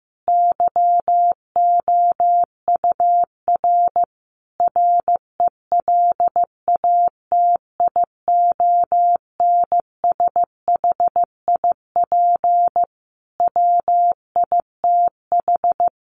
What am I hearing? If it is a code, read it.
Morse code, heard as a keyed tone, about 15 wpm, 700 Hz. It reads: YOUR RELATIONSHIP WITH
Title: Sentences from Top 1100 Words 15wpm